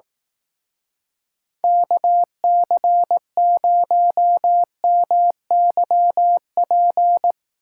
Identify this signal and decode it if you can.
Morse code, heard as a keyed tone, about 18 wpm, 700 Hz. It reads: KC0MYP
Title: Speed Racing - USA 2x3 Call Signs 15wpm